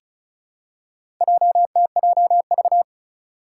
Morse code, heard as a keyed tone, about 36 wpm, 700 Hz. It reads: JTJV
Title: Four Challenging Letter-Number Combinations 35wpm